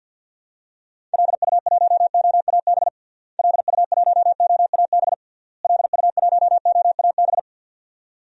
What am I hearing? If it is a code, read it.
Morse code, heard as a keyed tone, about 53 wpm, 700 Hz. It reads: LU1OAB LU1OAB LU1OAB
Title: DXCC-100 Call Signs - 6 Characters - Repeated 3x - 50wpm